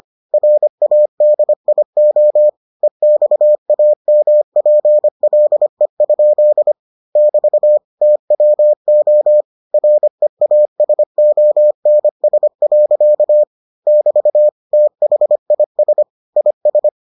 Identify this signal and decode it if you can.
Morse code, heard as a keyed tone, about 25 wpm, 600 Hz. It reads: RADIO EXAMPLE? <BT> TWO REASONS. <BT> THIS IS